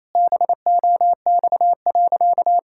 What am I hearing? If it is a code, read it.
Morse code, heard as a keyed tone, about 28 wpm, 700 Hz. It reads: BOX.
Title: Pangram Challenge v2 - 28wpm